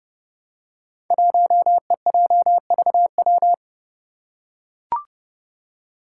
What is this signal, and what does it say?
Morse code, heard as a keyed tone, about 30 wpm, 700 Hz. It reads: 1EJVW
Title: Five Challenging Letter-Number Combinations 30wpm